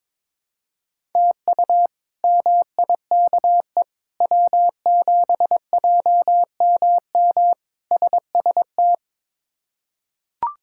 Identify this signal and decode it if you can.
Morse code, heard as a keyed tone, about 22 wpm, 700 Hz. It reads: TU MIKE W7JMM SST
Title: K1USN SST QSO Practice - 22wpm